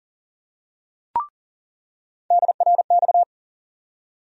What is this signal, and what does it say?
Morse code, heard as a keyed tone, about 39 wpm, 700 Hz. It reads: DRX
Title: USA Call Sign Postfixes - 3 Letters 40wpm